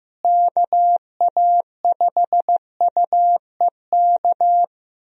Morse code, heard as a keyed tone, about 15 wpm, 700 Hz. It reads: KA5UEK
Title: USA Call Signs - 6 Characters - Repeated 3x - 15wpm